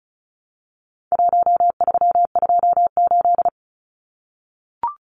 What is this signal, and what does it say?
Morse code, heard as a keyed tone, about 34 wpm, 700 Hz. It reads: 1328
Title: Numbers - Contest Style - 4-Digit 35wpm